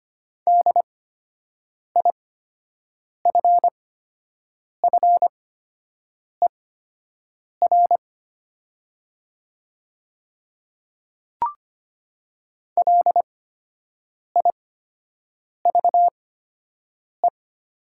Morse code, heard as a keyed tone, about 25 wpm, spaced as for 7 wpm, 700 Hz. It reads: DIFFER LIVE
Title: 8x Character Spacing - Top 200 Words - 25wpm